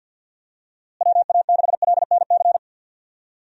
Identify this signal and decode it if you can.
Morse code, heard as a keyed tone, about 51 wpm, 700 Hz. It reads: WA6LNC
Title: USA Call Signs - 6 Characters - Repeated 3x - 50wpm